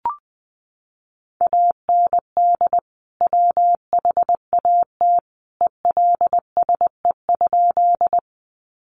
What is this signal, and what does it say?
Morse code, heard as a keyed tone, about 20 wpm, 700 Hz. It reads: AND WHAT ELSE?
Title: Sentences from Top 800 Words 20wpm